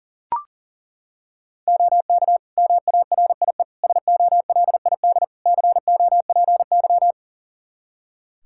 Morse code, heard as a keyed tone, about 40 wpm, 700 Hz. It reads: OK MARIE SOLID COPY
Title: Practice QSO - Exchange 3 - More Info - 40wpm